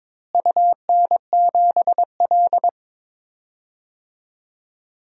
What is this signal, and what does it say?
Morse code, heard as a keyed tone, about 22 wpm, 700 Hz. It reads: UN7L
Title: DXCC-100 Call Signs - 4 Characters - Repeated 2x - 22wpm